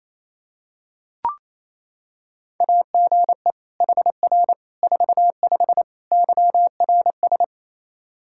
Morse code, heard as a keyed tone, about 28 wpm, 700 Hz. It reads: AGE HR 45 YRS
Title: Practice QSO - Exchange 3 - More Info - 28wpm